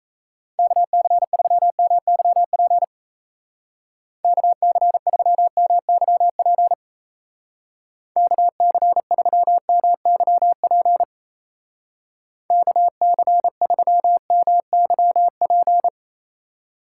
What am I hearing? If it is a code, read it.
Morse code, heard as a keyed tone, about 34 wpm, 700 Hz. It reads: KC3MYP KC3MYP KC3MYP KC3MYP
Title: Speed Racing - USA 2x3 Call Signs 28wpm